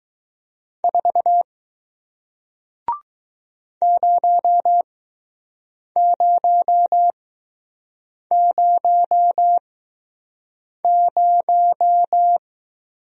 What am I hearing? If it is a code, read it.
Morse code, heard as a keyed tone, about 20 wpm, spaced as for 6 wpm, 700 Hz. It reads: 4 0000